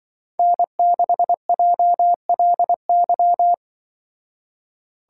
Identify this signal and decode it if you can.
Morse code, heard as a keyed tone, about 24 wpm, 700 Hz. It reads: N6JLY
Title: Speed Racing - USA Call Signs 20wpm